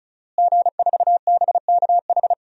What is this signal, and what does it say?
Morse code, heard as a keyed tone, about 36 wpm, 700 Hz. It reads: G4BKH
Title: DXCC-100 Call Signs - 5 Characters - Repeated 3x - 35wpm